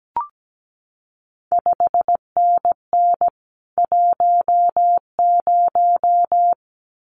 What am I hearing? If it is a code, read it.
Morse code, heard as a keyed tone, about 17 wpm, 700 Hz. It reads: 5NN 10
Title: CQ WW DX Contest Exchange Encore - 17wpm